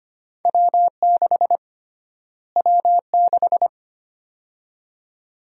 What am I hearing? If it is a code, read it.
Morse code, heard as a keyed tone, about 25 wpm, 700 Hz. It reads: W6 W6
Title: USA Call Sign Prefixes - Repeated 2x - 25wpm